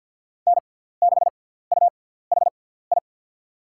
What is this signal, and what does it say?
Morse code, heard as a keyed tone, about 52 wpm, spaced as for 17 wpm, 700 Hz. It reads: N6UHI
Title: USA Call Signs - 6x Character Spacing - 50wpm